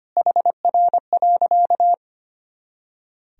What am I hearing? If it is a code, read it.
Morse code, heard as a keyed tone, about 25 wpm, 700 Hz. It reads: HR.
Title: Sentences from Top 400 Words and Abbreviations 25wpm